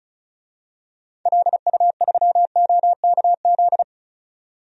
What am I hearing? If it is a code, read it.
Morse code, heard as a keyed tone, about 34 wpm, 700 Hz. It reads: LU3OKZ
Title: DXCC-100 Call Signs - 6 Characters - 35wpm